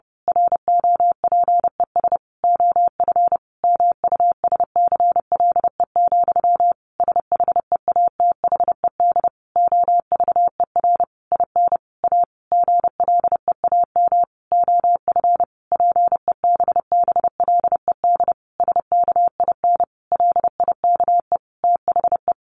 Morse code, heard as a keyed tone, about 30 wpm, 700 Hz. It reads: ROPES OF MUSCLE, SHEATHED OVER IN A GLEAM OF PEBBLED SKIN LIKE THE
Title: A Sound Of Thunder 30wpm